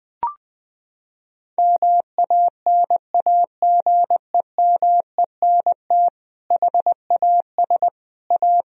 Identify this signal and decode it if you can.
Morse code, heard as a keyed tone, about 19 wpm, 700 Hz. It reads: MANAGEMENT HAS A
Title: Sentences from Top 1000 Words - Review Part 4 - 20wpm